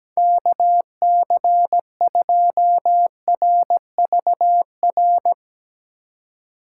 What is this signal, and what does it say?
Morse code, heard as a keyed tone, about 18 wpm, 700 Hz. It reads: KC2RVR
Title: Speed Racing - USA Call Signs 17wpm